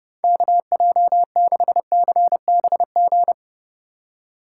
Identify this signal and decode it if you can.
Morse code, heard as a keyed tone, about 30 wpm, 700 Hz. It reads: KJ6CBG
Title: USA Call Signs - 6 Characters - 4x Character Spacing - 30wpm